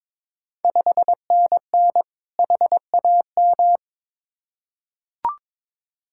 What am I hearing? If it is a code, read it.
Morse code, heard as a keyed tone, about 22 wpm, 690 Hz. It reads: 5NN HAM
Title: Ontario QSO Party Exchange 22wpm